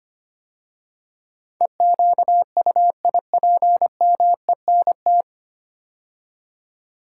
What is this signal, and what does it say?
Morse code, heard as a keyed tone, about 25 wpm, 700 Hz. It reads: EQUIPMENT
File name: 9 Letter Words - First 3 Letters Given - 25wpm